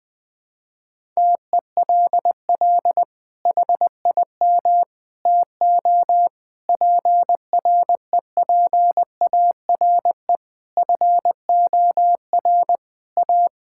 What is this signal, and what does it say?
Morse code, heard as a keyed tone, about 20 wpm, 700 Hz. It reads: TELL HIM TO PREPARE FOR A
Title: Sentences from Top 1000 Words - Review Part 6 - 20wpm